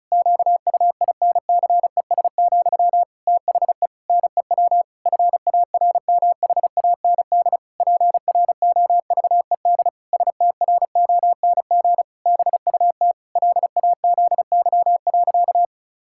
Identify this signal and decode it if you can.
Morse code, heard as a keyed tone, about 36 wpm, 700 Hz. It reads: QUINCES, THE NEW FARMHAND PROVED STRONG BUT LAZY.
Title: Pangram Challenge v2 - 35wpm